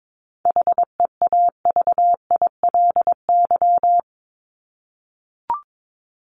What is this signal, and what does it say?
Morse code, heard as a keyed tone, about 22 wpm, 700 Hz. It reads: HEAVILY
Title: LY Suffix - Top 100 Words with Base Words 22wpm